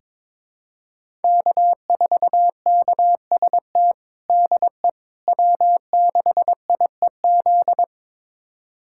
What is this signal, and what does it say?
Morse code, heard as a keyed tone, about 22 wpm, 700 Hz. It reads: K4KST DE W6IEZ
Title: USA Callsign DE Callsign - 5 Characters - 22wpm